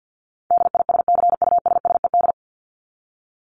Morse code, heard as a keyed tone, about 50 wpm, 700 Hz. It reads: DISCUSSED
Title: ED Suffix - Top 100 Words with Base Words 50wpm